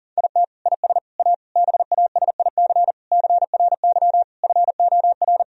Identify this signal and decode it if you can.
Morse code, heard as a keyed tone, about 40 wpm, 700 Hz. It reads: IT IS A BASIC CRY FOR
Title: Sentences from Top 1000 Words - Review Part 3 - 40wpm